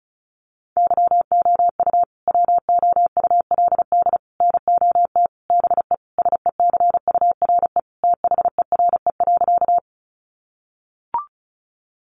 Morse code, heard as a keyed tone, about 34 wpm, 700 Hz. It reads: YOU WOULD NOT BE SECURE THERE.
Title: Sentences from Top 1200 Words 35wpm